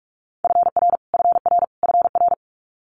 Morse code, heard as a keyed tone, about 46 wpm, 700 Hz. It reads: FR FR FR
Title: Multi-letter Phonemes Repeated 3x 45wpm